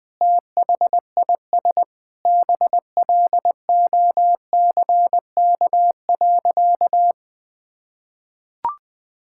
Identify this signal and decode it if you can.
Morse code, heard as a keyed tone, about 20 wpm, 700 Hz. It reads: THIS BLOCK.
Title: Sentences from Top 1600 Words 20wpm